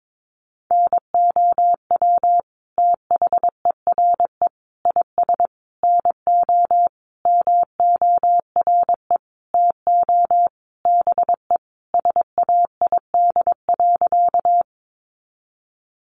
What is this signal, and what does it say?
Morse code, heard as a keyed tone, about 22 wpm, 700 Hz. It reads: NOW THERE IS NO MORE TO BE SAID.
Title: Sentences from Top 100 Words 22wpm